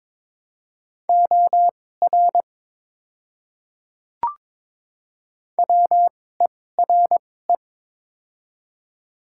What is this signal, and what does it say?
Morse code, heard as a keyed tone, about 22 wpm, spaced as for 16 wpm, 700 Hz. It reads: OR WERE
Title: 2x Character Spacing - Top 100 Words - 22wpm